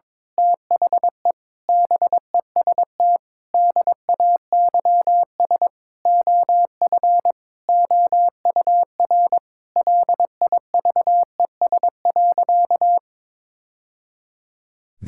Morse code, heard as a keyed tone, about 22 wpm, 700 Hz. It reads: THE BEST DAYS OF OUR LIVES.
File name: practice / 22wpm / Sentences with 2-Word Phrases - 22wpm